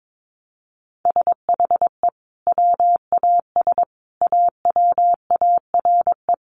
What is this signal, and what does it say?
Morse code, heard as a keyed tone, about 22 wpm, 700 Hz. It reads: SHE WAS AWARE
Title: Sentences from Top 1200 Words 22wpm